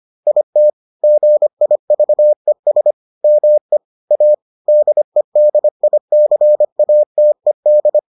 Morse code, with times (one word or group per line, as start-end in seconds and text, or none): 0.3-0.7 IT
1.0-2.9 GIVES
3.2-3.8 ME
4.1-4.3 A
4.7-8.0 DEDICATED